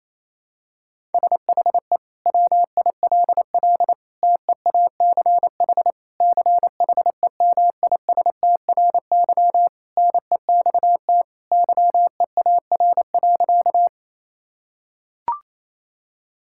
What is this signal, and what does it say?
Morse code, heard as a keyed tone, about 28 wpm, 700 Hz. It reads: SHE WILL TEACH CHEMISTRY NEXT YEAR.
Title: Sentences from Top 1000 Words - Review Part 6 - 28wpm